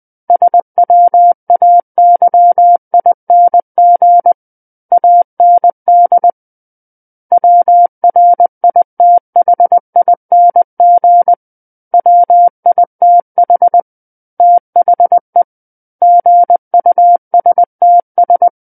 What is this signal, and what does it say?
Morse code, heard as a keyed tone, about 20 wpm, 700 Hz. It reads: SWAYING AND WRITHING WITH THE GUSTS
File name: 20 WPM